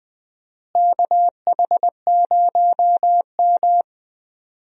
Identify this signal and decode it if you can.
Morse code, heard as a keyed tone, about 21 wpm, 700 Hz. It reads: KH0M
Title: USA Call Signs - 4 Characters - Repeated 2x - 20wpm